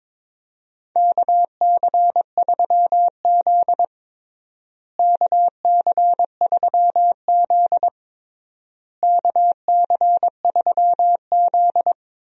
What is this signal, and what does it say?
Morse code, heard as a keyed tone, about 21 wpm, 700 Hz. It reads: KC3Z KC3Z KC3Z
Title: USA Call Signs - 4 Characters - Repeated 3x - 22wpm